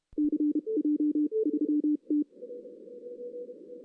Morse code, heard as a keyed tone, about 31 wpm, spaced as for 27 wpm, 300 Hz. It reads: CJ3T
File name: 2016 arrl dx cw / 2016-ARRL-DX-CW-K5TR- - 0-3